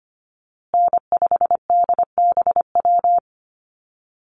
Morse code, heard as a keyed tone, about 25 wpm, 700 Hz. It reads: N5DBW
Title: USA Call Signs Repeated 2x 25wpm